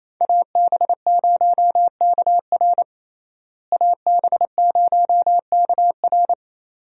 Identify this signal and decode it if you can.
Morse code, heard as a keyed tone, about 28 wpm, 700 Hz. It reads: AB0KR AB0KR
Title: USA Call Signs - 5 Characters - Repeated 2x - 28wpm